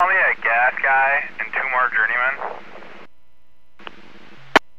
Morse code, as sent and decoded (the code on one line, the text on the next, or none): none